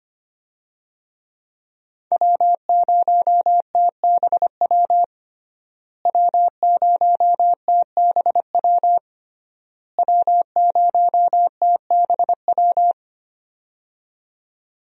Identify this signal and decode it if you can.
Morse code, heard as a keyed tone, about 25 wpm, 700 Hz. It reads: W0TBW W0TBW W0TBW
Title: USA Call Signs - 5 Characters - Repeated 3x - 25wpm